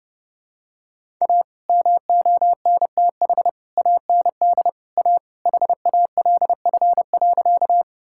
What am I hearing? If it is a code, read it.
Morse code, heard as a keyed tone, about 28 wpm, 700 Hz. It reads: A MONTH AND A HALF.
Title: Sentences from Top 1000 Words - Review Part 5 - 30wpm